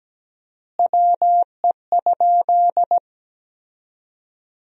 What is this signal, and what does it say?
Morse code, heard as a keyed tone, about 17 wpm, 700 Hz. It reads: WE?
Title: 4x Word-Spacing - Sentences from Top 200 Words - 17wpm